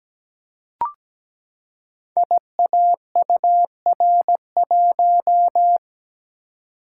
Morse code, heard as a keyed tone, about 17 wpm, 700 Hz. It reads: IAUR1
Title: Code Groups - 5 Letter:Numbers - 17wpm